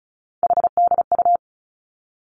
Morse code, heard as a keyed tone, about 36 wpm, 700 Hz. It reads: HDU
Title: Three Challenging Letter-Number Combinations 35wpm